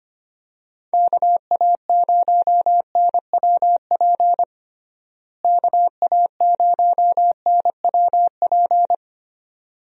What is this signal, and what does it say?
Morse code, heard as a keyed tone, about 25 wpm, 700 Hz. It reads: KA0NWP KA0NWP
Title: USA Call Signs - 6 Characters - Repeated 2x - 25wpm